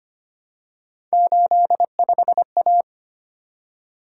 Morse code, heard as a keyed tone, about 25 wpm, 700 Hz. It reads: 85A
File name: Three Letter-Number Combinations - 25wpm